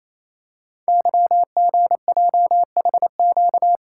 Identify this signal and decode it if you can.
Morse code, heard as a keyed tone, about 27 wpm, 700 Hz. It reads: YGJHQ
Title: Code Groups - 5 Letters - 28wpm